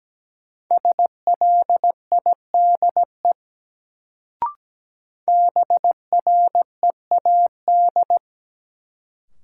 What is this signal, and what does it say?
Morse code, heard as a keyed tone, about 17 wpm, 700 Hz. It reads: SLIDE BREAD
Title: Sentences from Top 1600 Words - New Word Drill - 17wpm